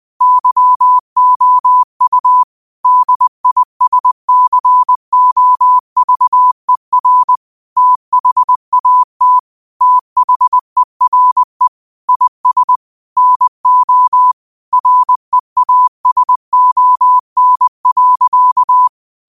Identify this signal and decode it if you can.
Morse code, heard as a keyed tone, about 20 wpm, 1000 Hz. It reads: YOU DISCOVER THAT THERE IS NO REASON.